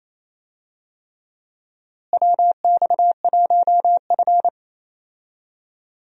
Morse code, heard as a keyed tone, about 28 wpm, 700 Hz. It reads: WX1F
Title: USA Call Signs - 4 Characters - Repeated 3x - 28wpm